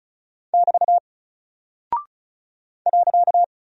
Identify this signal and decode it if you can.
Morse code, heard as a keyed tone, about 34 wpm, 700 Hz. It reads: <BT> .
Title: Single Character (Letter:Number:Punctuation:Prosign) Alt 35wpm